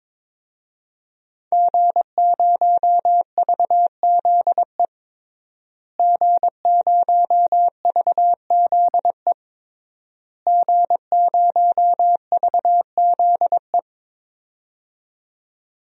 Morse code, heard as a keyed tone, about 22 wpm, 700 Hz. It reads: G0VZE G0VZE G0VZE
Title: DXCC-100 Call Signs - 5 Characters - Repeated 3x - 22wpm